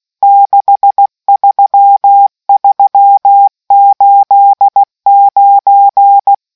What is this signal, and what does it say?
Morse code, heard as a keyed tone, about 16 wpm, 800 Hz. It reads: 63389